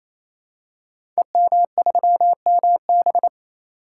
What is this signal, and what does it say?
Morse code, heard as a keyed tone, about 28 wpm, 700 Hz. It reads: EM3MB
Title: Code Groups - 5 Letter:Numbers - 28wpm